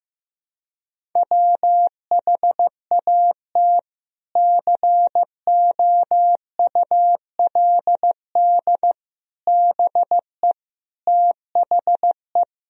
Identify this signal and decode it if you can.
Morse code, heard as a keyed tone, about 15 wpm, 700 Hz. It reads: WHAT COULD BE THE
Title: Sentences from Top 1000 Words - Review Part 1 - 15wpm